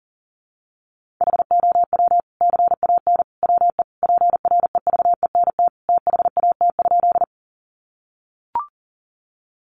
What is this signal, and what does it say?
Morse code, heard as a keyed tone, about 40 wpm, 700 Hz. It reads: HOW CAN WE PREVENT THAT?
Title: Sentences from Top 1100 Words 40wpm